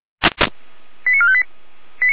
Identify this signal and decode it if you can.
Five-tone selective call (ZVEI), heard as quick runs of five tones, tones of about 70 ms. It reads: 89458
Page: /455/SDR/ZVEI/